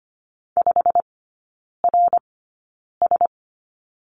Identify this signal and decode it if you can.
Morse code, heard as a keyed tone, about 25 wpm, spaced as for 9 wpm, 700 Hz. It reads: 5RS
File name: course / 25:10wpm / 093 Lesson 16e - Letter c - Three Characters 25wpm